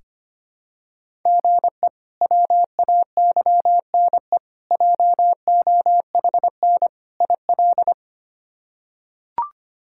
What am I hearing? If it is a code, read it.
Morse code, heard as a keyed tone, about 25 wpm, 700 Hz. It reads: GE WAYNE JOHN IL
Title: K1USN SST QSO Practice - 25wpm